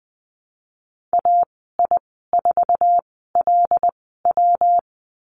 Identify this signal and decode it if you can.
Morse code, heard as a keyed tone, about 20 wpm, spaced as for 15 wpm, 700 Hz. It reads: AI4LW
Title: 140 Lesson 23g - Letter f - Call Signs 20wpm